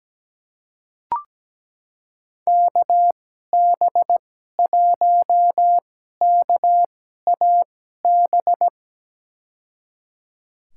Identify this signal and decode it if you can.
Morse code, heard as a keyed tone, about 17 wpm, spaced as for 12 wpm, 700 Hz. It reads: KB1KAB
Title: USA Call Signs - 6 Characters - 2x Character Spacing - 17wpm